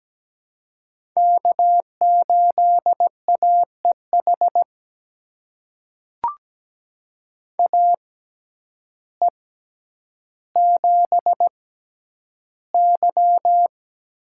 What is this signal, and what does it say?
Morse code, heard as a keyed tone, about 17 wpm, spaced as for 9 wpm, 700 Hz. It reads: K8AEH AE7Y
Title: USA Call Signs - 5 Characters - 6x Character Spacing - 17wpm